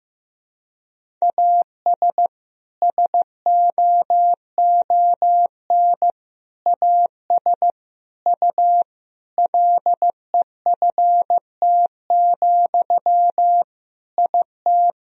Morse code, heard as a keyed tone, about 15 wpm, 700 Hz. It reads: AS SOON AS U LEFT, IT
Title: Sentences from Top 400 Words and Abbreviations 15wpm